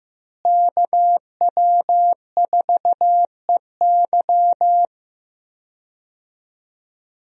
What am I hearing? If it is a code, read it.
Morse code, heard as a keyed tone, about 15 wpm, 700 Hz. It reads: KW4EY
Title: USA Call Signs - 5 Characters - Repeated 3x - 15wpm